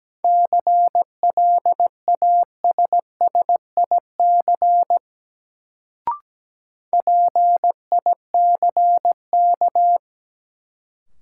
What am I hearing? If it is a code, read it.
Morse code, heard as a keyed tone, about 17 wpm, 700 Hz. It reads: CLASSIC PICK
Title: Sentences from Top 1000 Words - New Word Drill - 17wpm